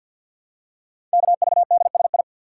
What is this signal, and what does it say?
Morse code, heard as a keyed tone, about 48 wpm, 700 Hz. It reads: KVDSI
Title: Code Groups - 5 Letters - 50wpm